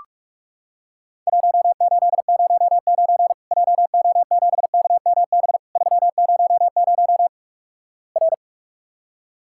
Morse code, heard as a keyed tone, about 46 wpm, 700 Hz. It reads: 1809 JO7KMB 300 R
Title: Worked All Europe DX Contest - QTC Exchange - 45wpm